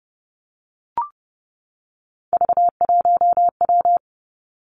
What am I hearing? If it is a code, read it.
Morse code, heard as a keyed tone, about 30 wpm, 700 Hz. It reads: V1W
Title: Three Challenging Letter-Number Combinations 30wpm